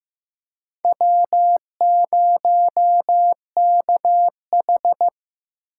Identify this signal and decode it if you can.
Morse code, heard as a keyed tone, about 15 wpm, 700 Hz. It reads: W0KH
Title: USA Call Signs - 4 Characters - 4x Character Spacing - 15wpm